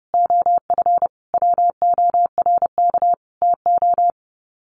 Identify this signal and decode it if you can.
Morse code, heard as a keyed tone, about 30 wpm, 700 Hz. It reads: OF WORK TO
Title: Sets of 5 Words Encore2 30wpm